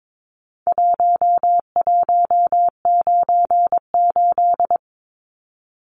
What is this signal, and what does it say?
Morse code, heard as a keyed tone, about 22 wpm, 700 Hz. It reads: 1198
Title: Numbers - Contest Style - 4-Digit 22wpm